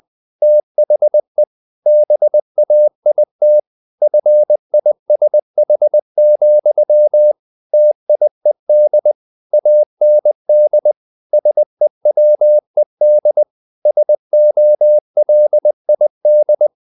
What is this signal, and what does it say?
Morse code, heard as a keyed tone, about 20 wpm, 600 Hz. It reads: THE BAIT FISH, TIED AND SEWED SOLID
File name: the old man and the sea / the old man and the sea part 1 20wpm